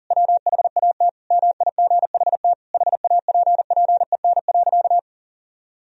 Morse code, heard as a keyed tone, about 37 wpm, 700 Hz. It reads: WHAT MIGHT HAPPEN.
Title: Sentences from Top 400 Words and Abbreviations 40wpm